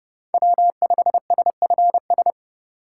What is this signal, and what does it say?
Morse code, heard as a keyed tone, about 30 wpm, 700 Hz. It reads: W5SFS
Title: USA Call Signs - 5 Characters - 4x Character Spacing - 30wpm